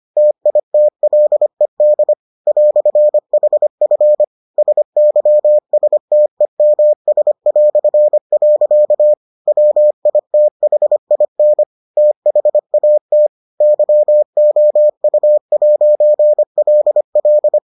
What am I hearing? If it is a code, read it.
Morse code, heard as a keyed tone, about 25 wpm, 600 Hz. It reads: TITLED "HF SYSTEMS". WITHIN THAT YOU'LL